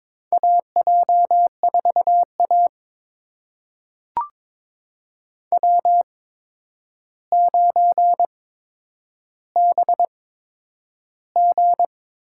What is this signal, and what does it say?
Morse code, heard as a keyed tone, about 22 wpm, spaced as for 9 wpm, 700 Hz. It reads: AJ4A W9BG
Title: USA Call Signs - 4 Characters - 8x Character Spacing - 22wpm